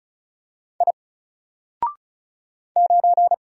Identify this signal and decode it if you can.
Morse code, heard as a keyed tone, about 36 wpm, 700 Hz. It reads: I 9
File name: Single-Letter-Number-35wpm